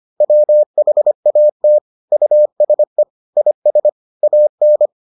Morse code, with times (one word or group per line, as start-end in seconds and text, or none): 0.2-1.8 WHAT
2.1-3.0 USE
3.4-3.9 IS
4.2-4.9 AN